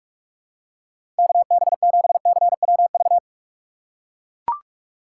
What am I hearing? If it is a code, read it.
Morse code, heard as a keyed tone, about 41 wpm, 700 Hz. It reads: KB7CWV
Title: USA Call Signs - 6 Characters - 2x Character Spacing - 45wpm